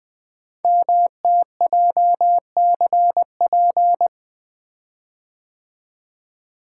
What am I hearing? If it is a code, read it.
Morse code, heard as a keyed tone, about 20 wpm, 700 Hz. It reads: MTJCP
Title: Code Groups - 5 Letters - 20wpm